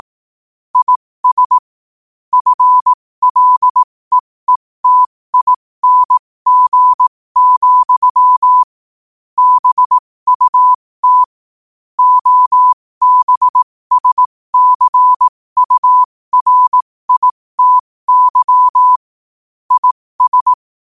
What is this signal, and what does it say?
Morse code, heard as a keyed tone, about 18 wpm, spaced as for 15 wpm, 1000 Hz. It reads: IS FLEETING, BUT OBSCURITY IS
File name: qotdfiles / 15wpm